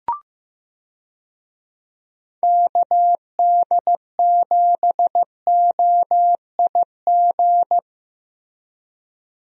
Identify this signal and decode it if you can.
Morse code, heard as a keyed tone, about 15 wpm, 700 Hz. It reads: KD7OIG
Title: USA Call Signs - 6 Characters - Repeated 3x - 15wpm